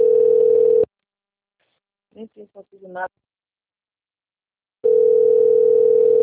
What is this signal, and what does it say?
Ringback tone (North America), 440 + 480 Hz, with speech.
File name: rg-1600-unknown- - - .5754